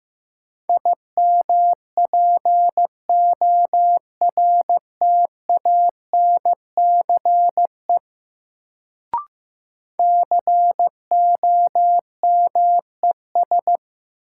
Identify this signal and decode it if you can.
Morse code, heard as a keyed tone, about 15 wpm, 700 Hz. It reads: IMPORTANCE COMES
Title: Sets of 3 Words - New Word Drill - 15wpm